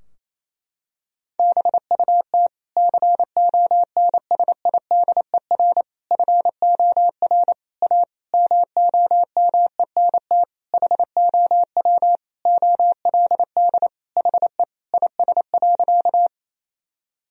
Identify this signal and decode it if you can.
Morse code, heard as a keyed tone, about 28 wpm, 700 Hz. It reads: BUT CONSIDER FOR A MOMENT HOW OLD HE IS.
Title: Sentences from Top 1000 Words - Review Part 4 - 28wpm